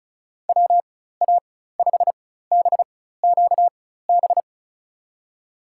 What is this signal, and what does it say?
Morse code, heard as a keyed tone, about 36 wpm, spaced as for 16 wpm, 700 Hz. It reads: WA5BQB
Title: USA Call Signs - 4x Character Spacing - 35wpm